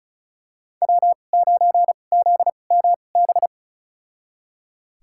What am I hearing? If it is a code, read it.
Morse code, heard as a keyed tone, about 36 wpm, spaced as for 25 wpm, 700 Hz. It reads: W9ZMB
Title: USA Call Signs - 5 Characters - 2x Character Spacing - 35wpm